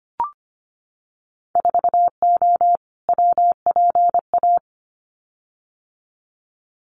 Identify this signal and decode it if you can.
Morse code, heard as a keyed tone, about 25 wpm, 700 Hz. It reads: 4O WPA
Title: Winter Field Day Exchanges 25wpm